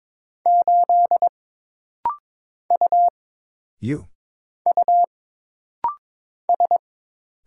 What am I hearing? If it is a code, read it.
Morse code, heard as a keyed tone, about 22 wpm, 700 Hz. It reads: 8 U U S
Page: This rapid-fire timing is geared to push your ICR (Instant Character Recognition) skills.